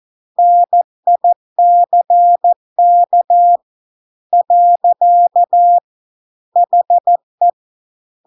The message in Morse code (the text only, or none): NICK . HE